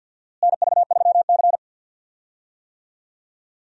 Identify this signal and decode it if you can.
Morse code, heard as a keyed tone, about 52 wpm, 700 Hz. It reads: NV3C
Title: USA Call Signs - 4 Characters - Repeated 2x - 50wpm